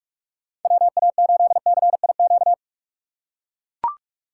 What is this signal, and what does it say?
Morse code, heard as a keyed tone, about 41 wpm, 700 Hz. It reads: WA8CIQ